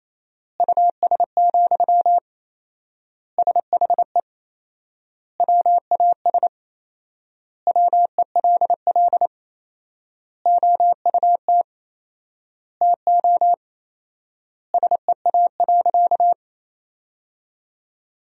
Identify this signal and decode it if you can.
Morse code, heard as a keyed tone, about 28 wpm, 700 Hz. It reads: US, SHE WAS WELL OUT TO SEA.
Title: 4x Word-Spacing - Sentences from Top 400 Words - 28wpm